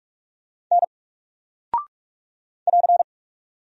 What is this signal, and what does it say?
Morse code, heard as a keyed tone, about 44 wpm, 700 Hz. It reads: N <AR>
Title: Single Character (Letter:Number:Punctuation:Prosign) Alt 45wpm